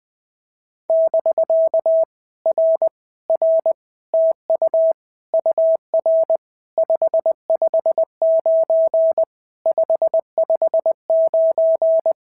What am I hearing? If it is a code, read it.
Morse code, heard as a keyed tone, about 20 wpm, 650 Hz. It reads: <BK> R R TU UR 559 559
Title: POTA QSO Practice - 20wpm